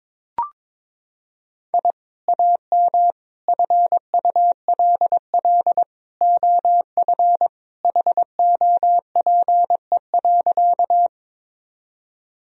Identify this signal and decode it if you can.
Morse code, heard as a keyed tone, about 22 wpm, 700 Hz. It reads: I AM FULL OF HOPE.
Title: Sentences from Top 1000 Words - Review Part 3 - 22wpm